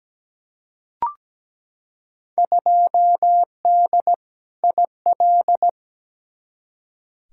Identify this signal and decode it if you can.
Morse code, heard as a keyed tone, about 17 wpm, 700 Hz. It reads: 2D IL
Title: Field Day Exchanges - 17wpm